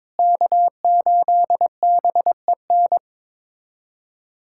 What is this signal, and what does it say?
Morse code, heard as a keyed tone, about 21 wpm, 700 Hz. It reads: K8BEN
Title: USA Callsign DE Callsign - 5 Characters - 22wpm